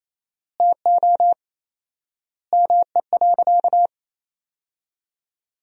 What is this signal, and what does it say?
Morse code, heard as a keyed tone, about 27 wpm, 700 Hz. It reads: TO ME.
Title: 4x Word-Spacing - Sentences from Top 400 Words - 28wpm